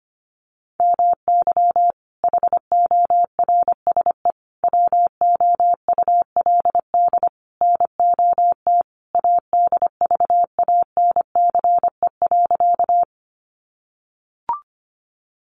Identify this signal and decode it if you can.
Morse code, heard as a keyed tone, about 25 wpm, 700 Hz. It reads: MY HORSE WOULD NOT ADVANCE.
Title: Sentences from Top 1400 Words 25wpm